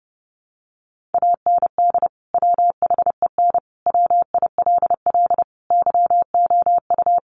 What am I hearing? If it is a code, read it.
Morse code, heard as a keyed tone, about 30 wpm, 700 Hz. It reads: AND WHEN WILL YOU
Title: Sentences from Top 100 Words 30wpm